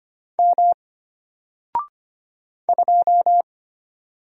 Morse code, heard as a keyed tone, about 25 wpm, 700 Hz. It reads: M 2
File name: Single-Letter-Number-25wpm